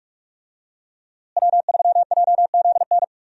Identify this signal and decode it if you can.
Morse code, heard as a keyed tone, about 43 wpm, 700 Hz. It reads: W3JZN
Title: USA Call Signs - 5 Characters - Repeated 2x - 45wpm